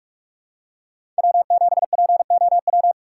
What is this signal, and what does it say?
Morse code, heard as a keyed tone, about 46 wpm, 700 Hz. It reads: W7POW
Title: USA Call Signs - 5 Characters - 4x Character Spacing - 45wpm